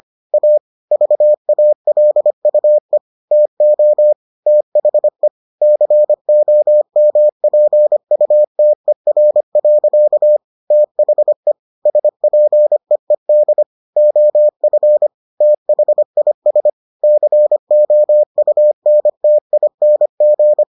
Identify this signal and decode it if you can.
Morse code, heard as a keyed tone, about 25 wpm, 600 Hz. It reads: A VALUE TO THE COMPUTER. THE SPEED OF THIS COUNTING